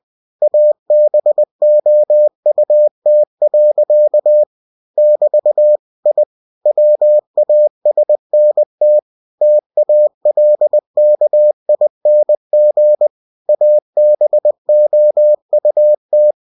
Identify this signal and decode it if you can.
Morse code, heard as a keyed tone, about 20 wpm, 600 Hz. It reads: ABOUT. <BT> I WASNT TALKING ABOUT
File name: cw-books / winnie-the-pooh / winnie-20wpm-page-0089